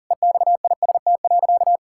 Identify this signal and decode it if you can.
Morse code, heard as a keyed tone, about 40 wpm, 700 Hz. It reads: EXIST.
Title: Sentences from Top 1000 Words - Review Part 5 - 40wpm